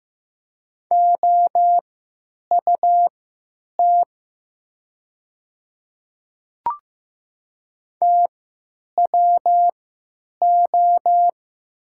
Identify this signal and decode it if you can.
Morse code, heard as a keyed tone, about 15 wpm, spaced as for 9 wpm, 700 Hz. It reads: OUT TWO
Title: 3x Character Spacing - Top 100 Words - 15wpm